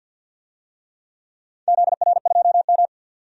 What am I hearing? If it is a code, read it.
Morse code, heard as a keyed tone, about 51 wpm, 700 Hz. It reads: ZR2M
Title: DXCC-100 Call Signs - 4 Characters - 50wpm